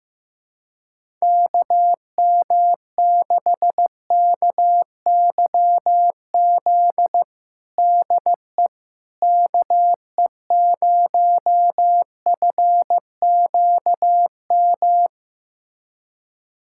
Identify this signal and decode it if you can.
Morse code, heard as a keyed tone, about 15 wpm, 700 Hz. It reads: KM6KYZ DE KE0FQM
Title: USA Callsign DE Callsign 15wpm